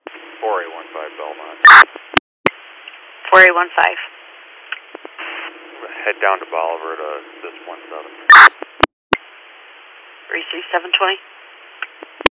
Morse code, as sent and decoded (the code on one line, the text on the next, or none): none